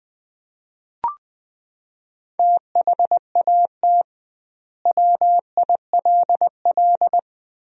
Morse code, heard as a keyed tone, about 21 wpm, 700 Hz. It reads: THAT WILL
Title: 2x Word-Spacing - Sentences from Top 100 Words - 20wpm